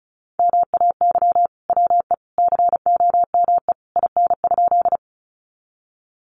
Morse code, heard as a keyed tone, about 34 wpm, 700 Hz. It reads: MAY WE COME IN?
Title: Sentences from Top 100 Words 35wpm